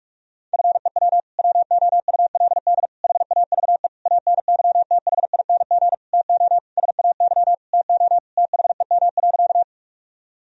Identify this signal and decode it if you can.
Morse code, heard as a keyed tone, about 46 wpm, 700 Hz. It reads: FEW WOULD HAVE ANYTHING TO SAY TO THEM.
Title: Sentences from Top 1000 Words - Review Part 1 - 45wpm